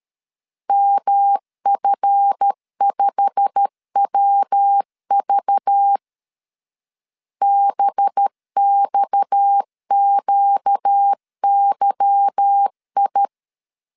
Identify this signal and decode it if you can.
Morse code, heard as a keyed tone, about 13 wpm, 800 Hz. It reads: MF5WV BXQYI
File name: LEZIONI / 8 YJ45 / 8 YJ45 12wpm 2151